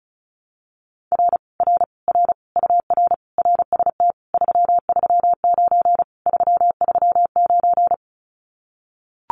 Morse code, heard as a keyed tone, about 36 wpm, 700 Hz. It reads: R R R UR RST 339 339
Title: Common QSO Phrases Encore 35wpm